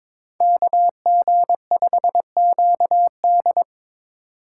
Morse code, heard as a keyed tone, about 26 wpm, 700 Hz. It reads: KG5QD
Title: Speed Racing - USA 2x2 Call Signs 22wpm